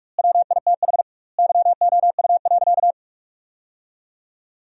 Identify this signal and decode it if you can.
Morse code, heard as a keyed tone, about 46 wpm, 700 Hz. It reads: WITH YOU.
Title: 2x Word-Spacing - Sentences from Top 300 Words - 45wpm